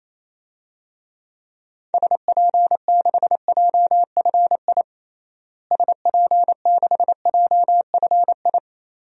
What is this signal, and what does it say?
Morse code, heard as a keyed tone, about 28 wpm, 700 Hz. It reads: SP6JFI SP6JFI
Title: DXCC-100 Call Signs - 6 Characters - Repeated 2x - 28wpm